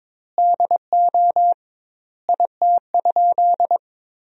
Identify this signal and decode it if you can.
Morse code, heard as a keyed tone, about 22 wpm, 700 Hz. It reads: DO IT?
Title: 2x Word-Spacing - Sentences from Top 100 Words - 22wpm